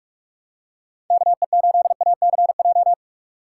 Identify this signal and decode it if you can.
Morse code, heard as a keyed tone, about 46 wpm, 700 Hz. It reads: KE8ACJ
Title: USA Call Signs - 3x Character Spacing - 45wpm